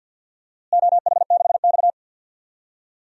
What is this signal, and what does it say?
Morse code, heard as a keyed tone, about 52 wpm, 700 Hz. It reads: OH6X
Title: DXCC-100 Call Signs - 4 Characters - Repeated 3x - 50wpm